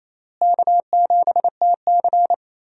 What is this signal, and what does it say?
Morse code, heard as a keyed tone, about 28 wpm, 700 Hz. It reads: K7TC
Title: USA Call Signs - 4 Characters - Repeated 2x - 28wpm